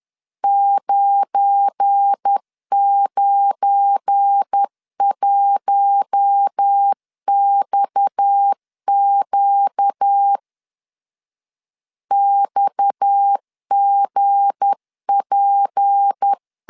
Morse code, heard as a keyed tone, about 10 wpm, 800 Hz. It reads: 991XQ XGP
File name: LEZIONI / 7 GF9Z / 7 GF9 10wpm 2747